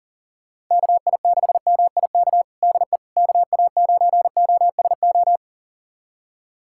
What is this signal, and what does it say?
Morse code, heard as a keyed tone, about 40 wpm, 700 Hz. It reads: KI6MIK DE KA9OSO
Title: USA Callsign DE Callsign - 6 Characters - 40wpm